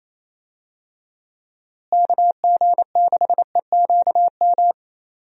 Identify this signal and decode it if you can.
Morse code, heard as a keyed tone, about 28 wpm, 700 Hz. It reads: KG6EQM
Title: USA Call Signs - 6 Characters - Repeated 3x - 28wpm